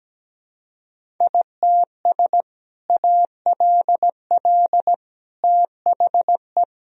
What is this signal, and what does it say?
Morse code, heard as a keyed tone, about 17 wpm, 700 Hz. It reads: ITS ALL THE
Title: Sentences from Top 1000 Words - Review Part 3 - 17wpm